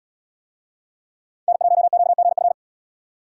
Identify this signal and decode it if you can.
Morse code, heard as a keyed tone, about 77 wpm, 700 Hz. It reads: N1QGF